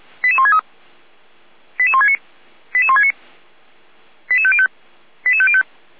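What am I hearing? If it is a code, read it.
Five-tone selective call (ZVEI), heard as quick runs of five tones, tones of about 70 ms. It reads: 88163 88169 88169 88505 88505
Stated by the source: ZVEI five-tone selective call received by SDR.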